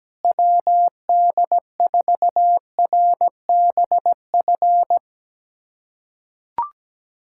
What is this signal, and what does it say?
Morse code, heard as a keyed tone, about 17 wpm, 700 Hz. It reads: WD4RBF
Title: USA Callsign DE Callsign - 6 Characters - 17wpm